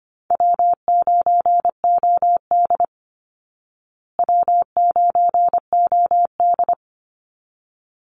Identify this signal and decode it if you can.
Morse code, heard as a keyed tone, about 25 wpm, 700 Hz. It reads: W9OD W9OD
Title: Easy Call Signs Repeated 2x - 25wpm